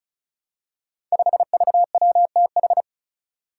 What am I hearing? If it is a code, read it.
Morse code, heard as a keyed tone, about 36 wpm, 700 Hz. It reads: 5VWTH
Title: Five Challenging Letter-Number Combinations 35wpm